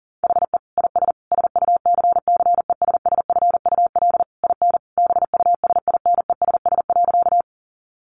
Morse code, heard as a keyed tone, about 40 wpm, 700 Hz. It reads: HE IS SUCCESSFUL IN BUSINESS.
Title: Sentences from Top 900 Words 40wpm